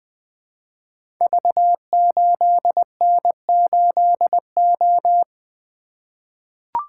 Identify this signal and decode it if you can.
Morse code, heard as a keyed tone, about 20 wpm, 700 Hz. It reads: V8N8O
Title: Code Groups - 5 Letter:Numbers - 20wpm